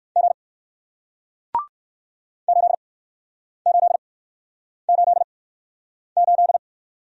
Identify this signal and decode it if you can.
Morse code, heard as a keyed tone, about 62 wpm, spaced as for 9 wpm, 700 Hz. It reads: G 8888